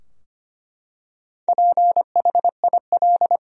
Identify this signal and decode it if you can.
Morse code, heard as a keyed tone, about 25 wpm, 700 Hz. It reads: PHIL
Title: USA Common Names - Max 4 Letters - 25wpm